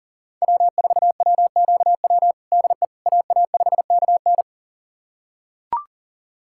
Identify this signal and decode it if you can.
Morse code, heard as a keyed tone, about 40 wpm, 700 Hz. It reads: W4WQW DE AA5KN
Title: USA Callsign DE Callsign - 5 Characters - 40wpm